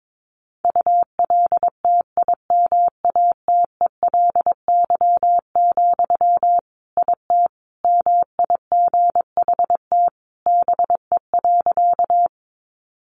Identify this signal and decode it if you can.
Morse code, heard as a keyed tone, about 22 wpm, 700 Hz. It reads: ULTIMATELY, IT MIGHT BE.
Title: Sentences from Top 1200 Words 22wpm